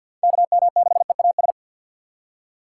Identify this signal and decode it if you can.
Morse code, heard as a keyed tone, about 51 wpm, 700 Hz. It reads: KM6EAS
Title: USA Call Signs - 6 Characters - 6x Character Spacing - 50wpm